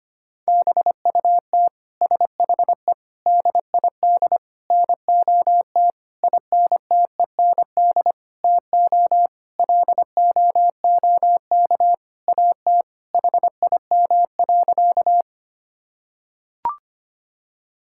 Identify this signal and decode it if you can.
Morse code, heard as a keyed tone, about 25 wpm, 700 Hz. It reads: BUT SHE DID NOT INTEND TO LOOK AT HIM.
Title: Sentences from Top 1600 Words 25wpm